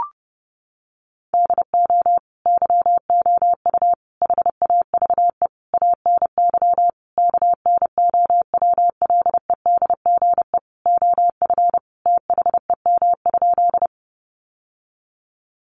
Morse code, heard as a keyed tone, about 30 wpm, 700 Hz. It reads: DO YOU HAVE ANY KNOWLEDGE OF THEM?
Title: Sentences from Top 600 Words 30wpm